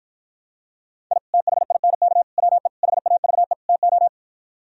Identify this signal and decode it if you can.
Morse code, heard as a keyed tone, about 53 wpm, 700 Hz. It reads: I THINK WE HAVE TO